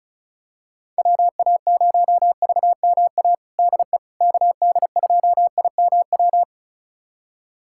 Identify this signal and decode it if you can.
Morse code, heard as a keyed tone, about 34 wpm, 700 Hz. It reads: WA0VMA DE KD2IMW
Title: USA Callsign DE Callsign 35wpm